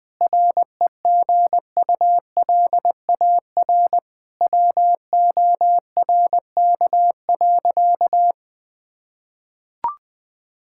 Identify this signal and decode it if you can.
Morse code, heard as a keyed tone, about 20 wpm, 700 Hz. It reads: REGULAR WORK.
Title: Sentences from Top 1000 Words - Review Part 6 - 20wpm